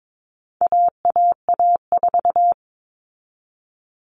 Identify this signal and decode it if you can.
Morse code, heard as a keyed tone, about 22 wpm, 700 Hz. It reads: AAA4
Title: All Cut Numbers - 4-Digit 22wpm